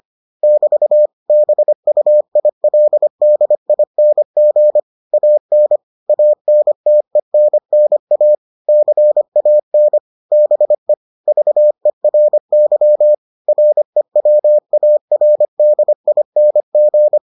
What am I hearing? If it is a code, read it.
Morse code, heard as a keyed tone, about 25 wpm, 600 Hz. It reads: <BT> BUILDING AN ANTENNA CAN BE VERY REWARDING